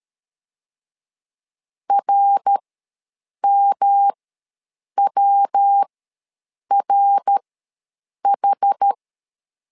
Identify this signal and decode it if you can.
Morse code, heard as a keyed tone, about 13 wpm, spaced as for 7 wpm, 800 Hz. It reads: RMWRH
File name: LEZIONI / 3 MWRO / 3 MWR ws 12wpm 1809